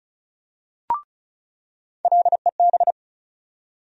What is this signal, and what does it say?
Morse code, heard as a keyed tone, about 34 wpm, 700 Hz. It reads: LEB
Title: USA Call Sign Postfixes - 3 Letters 35wpm